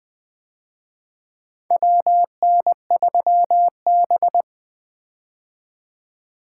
Morse code, heard as a keyed tone, about 24 wpm, 700 Hz. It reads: WN3B
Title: Speed-Racing - Four Letter-Number Combinations - 20wpm